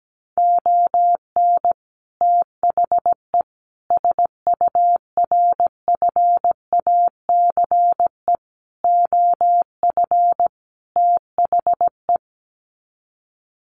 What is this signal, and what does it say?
Morse code, heard as a keyed tone, about 17 wpm, 700 Hz. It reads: ON THE SURFACE OF THE
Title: Sets of 5 Words Encore 17wpm